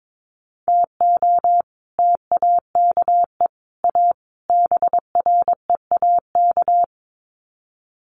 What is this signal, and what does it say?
Morse code, heard as a keyed tone, about 22 wpm, 700 Hz. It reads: TO TAKE A BREAK
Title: Sets of 4 Words 22wpm